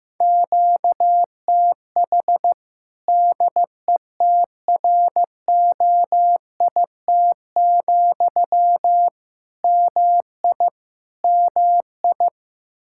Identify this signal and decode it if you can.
Morse code, heard as a keyed tone, about 15 wpm, 700 Hz. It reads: QTH DETROIT, MI MI
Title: Practice QSO - Exchange 1 - Beginning - 15wpm